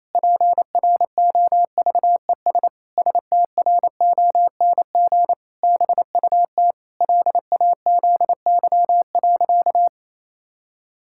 Morse code, heard as a keyed tone, about 28 wpm, 700 Hz. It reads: PROVES STRONG BUT LAZY.
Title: Pangram Challenge v2 - 28wpm